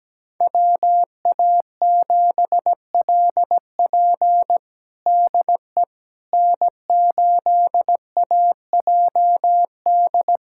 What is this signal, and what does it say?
Morse code, heard as a keyed tone, about 17 wpm, 700 Hz. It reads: WA7LP DE N8AJD
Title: USA Callsign DE Callsign - 5 Characters - 17wpm